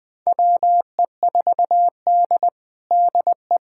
Morse code, heard as a keyed tone, about 21 wpm, 700 Hz. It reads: WE4D DE
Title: USA Callsign DE Callsign - 4 Characters - 20wpm